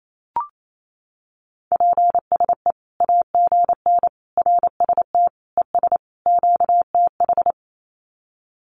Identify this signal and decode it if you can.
Morse code, heard as a keyed tone, about 28 wpm, 700 Hz. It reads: PSE AGN RST ES QTH
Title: Common QSO Phrases 28wpm